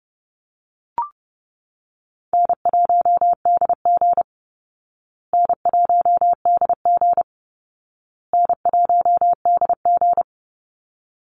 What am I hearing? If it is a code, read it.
Morse code, heard as a keyed tone, about 30 wpm, 700 Hz. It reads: N1DG N1DG N1DG
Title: Easy Call Signs Repeated 3x - 30wpm